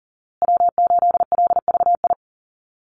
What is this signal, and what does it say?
Morse code, heard as a keyed tone, about 40 wpm, 700 Hz. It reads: W8LVI
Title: Call Signs Repeated 2x - 40wpm